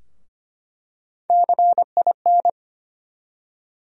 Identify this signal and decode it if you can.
Morse code, heard as a keyed tone, about 25 wpm, 700 Hz. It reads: CIN
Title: USA Call Sign Postfixes - 3 Letters 25wpm